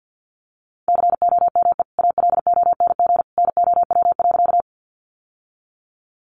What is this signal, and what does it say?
Morse code, heard as a keyed tone, about 50 wpm, 700 Hz. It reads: COME ALONG NOW.
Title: Sentences from Top 700 Words 50wpm